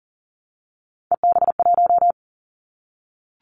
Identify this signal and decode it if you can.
Morse code, heard as a keyed tone, about 38 wpm, 700 Hz. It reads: EB1
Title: Callsign Prefixes for Top 25 DXCC - 40wpm